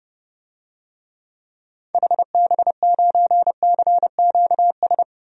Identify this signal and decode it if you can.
Morse code, heard as a keyed tone, about 30 wpm, 700 Hz. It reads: HB9CQS
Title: DXCC-100 Call Signs - 6 Characters - 30wpm